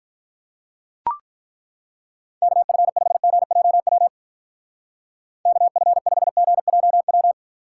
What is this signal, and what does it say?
Morse code, heard as a keyed tone, about 42 wpm, 700 Hz. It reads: KF5GJW KF5GJW
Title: Speed Racing - USA 2x3 Call Signs 35wpm